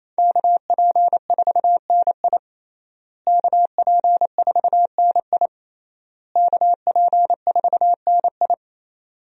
Morse code, heard as a keyed tone, about 28 wpm, 700 Hz. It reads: KP4NI KP4NI KP4NI
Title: USA Call Signs - 5 Characters - Repeated 3x - 28wpm